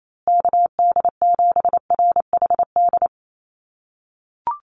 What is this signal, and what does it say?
Morse code, heard as a keyed tone, about 28 wpm, 700 Hz. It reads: KD7RHD
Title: USA Call Signs Repeated 3x 28wpm